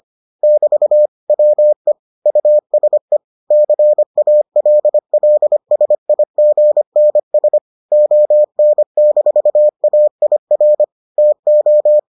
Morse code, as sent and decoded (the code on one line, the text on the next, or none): -...- / .-- . / ..- ... / -.-. .- .-.. .-.. ... .. --. -. ... / --- -. -....- .- .. .-. / - ---
<BT> WE USE CALLSIGNS ON-AIR TO